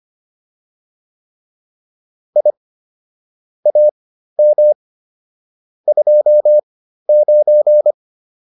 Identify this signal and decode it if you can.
Morse code, heard as a keyed tone, about 25 wpm, spaced as for 13 wpm, 600 Hz. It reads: I AM 29
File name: QSO303 13